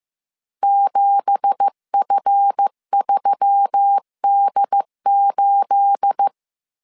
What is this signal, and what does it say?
Morse code, heard as a keyed tone, about 15 wpm, 800 Hz. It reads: 7F3D8
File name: CW / DET / 1 9 1500C S C 14wpm 3245